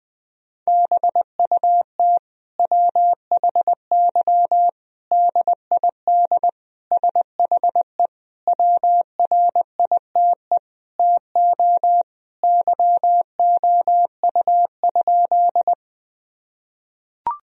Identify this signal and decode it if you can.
Morse code, heard as a keyed tone, about 21 wpm, 700 Hz. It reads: BUT WHY DID SHE WRITE TO YOU?
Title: Sentences from Top 1000 Words - Review Part 1 - 20wpm